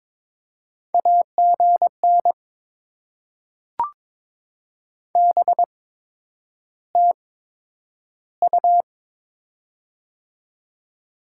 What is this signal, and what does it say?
Morse code, heard as a keyed tone, about 22 wpm, spaced as for 9 wpm, 700 Hz. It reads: AGN BTU
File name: Top 20 QSO Elements - 8x Character Spacing - 22wpm